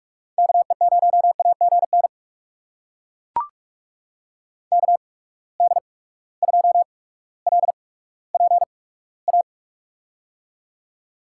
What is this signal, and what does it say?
Morse code, heard as a keyed tone, about 41 wpm, spaced as for 19 wpm, 700 Hz. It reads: KE0AGN KD2LPA
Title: USA Call Signs - 6 Characters - 8x Character Spacing - 45wpm